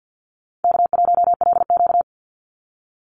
Morse code, heard as a keyed tone, about 50 wpm, 700 Hz. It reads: K1LQ
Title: USA Call Signs 50wpm